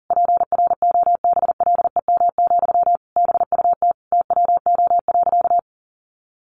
Morse code, heard as a keyed tone, about 37 wpm, 700 Hz. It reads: PROBLEM, BUT TWO.
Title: Sentences from Top 1400 Words 40wpm